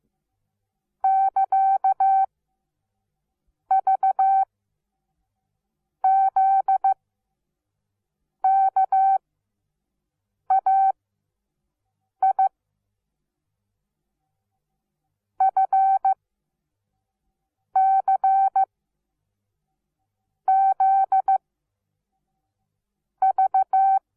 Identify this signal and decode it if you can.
Morse code, heard as a keyed tone, about 15 wpm, spaced as for 6 wpm, 770 Hz. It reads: <KA> V Z KAI F C Z V
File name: Onlinelehrgang / morsen / DJ4UF Morsen17